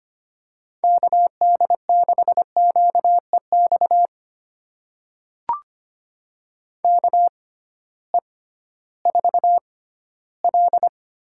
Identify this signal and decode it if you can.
Morse code, heard as a keyed tone, about 25 wpm, spaced as for 15 wpm, 700 Hz. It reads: KD6QEX KE4L
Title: USA Call Signs - 6 Characters - 6x Character Spacing - 25wpm